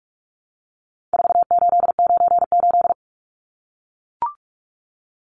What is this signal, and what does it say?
Morse code, heard as a keyed tone, about 44 wpm, 700 Hz. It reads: 4898
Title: Numbers - 4-Digit 45wpm